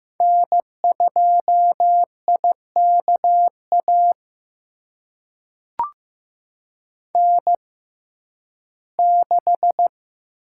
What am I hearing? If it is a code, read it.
Morse code, heard as a keyed tone, about 15 wpm, spaced as for 11 wpm, 700 Hz. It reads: N2IKA N6
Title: USA Call Signs - 5 Characters - 6x Character Spacing - 15wpm